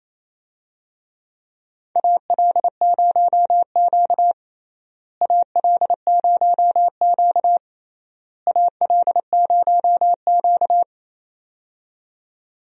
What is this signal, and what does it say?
Morse code, heard as a keyed tone, about 29 wpm, 700 Hz. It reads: AL0Q AL0Q AL0Q
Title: USA Call Signs - 4 Characters - Repeated 3x - 28wpm